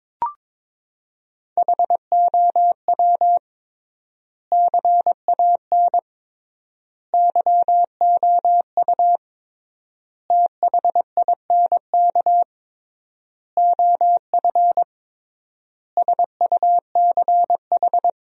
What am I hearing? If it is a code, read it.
Morse code, heard as a keyed tone, about 22 wpm, 700 Hz. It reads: HOW CAN YOU THINK OF SUCH
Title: 3x Word-Spacing - Sentences from Top 200 Words - 22wpm